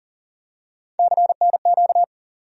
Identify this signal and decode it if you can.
Morse code, heard as a keyed tone, about 37 wpm, 700 Hz. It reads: CNQ
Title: USA Call Sign Postfixes - 3 Letters 40wpm